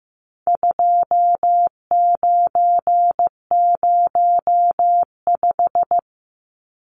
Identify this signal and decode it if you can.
Morse code, heard as a keyed tone, about 15 wpm, 700 Hz. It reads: 2905
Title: Numbers - 4-Digit 15wpm